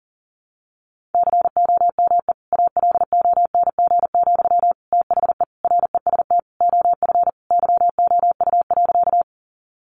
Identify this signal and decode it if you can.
Morse code, heard as a keyed tone, about 40 wpm, 700 Hz. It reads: COME ALONG, THE REST OF YOU.
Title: Sentences from Top 700 Words 40wpm